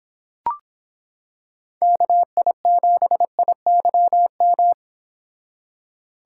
Morse code, heard as a keyed tone, about 26 wpm, 700 Hz. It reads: KI7IYM